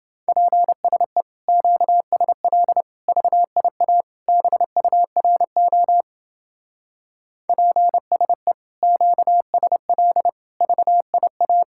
Morse code, heard as a keyed tone, about 30 wpm, 700 Hz. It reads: PSE QSL VIA BURO PSE QSL VIA